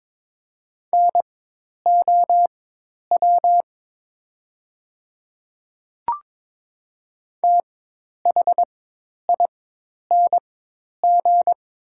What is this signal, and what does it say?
Morse code, heard as a keyed tone, about 22 wpm, spaced as for 10 wpm, 700 Hz. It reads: NOW THING
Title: 4x Character Spacing - Top 200 Words - 22wpm